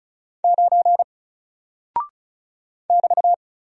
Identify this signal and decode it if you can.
Morse code, heard as a keyed tone, about 36 wpm, 700 Hz. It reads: 9 <BT>
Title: Single Character (Letter:Number:Punctuation:Prosign) Alt 35wpm